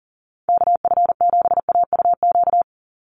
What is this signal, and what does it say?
Morse code, heard as a keyed tone, about 40 wpm, 700 Hz. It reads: KF7AUQ
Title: USA Call Signs 40wpm